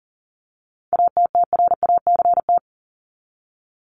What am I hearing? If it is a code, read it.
Morse code, heard as a keyed tone, about 40 wpm, 700 Hz. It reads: ATTRACT
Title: Top 1501-1750 Words 40wpm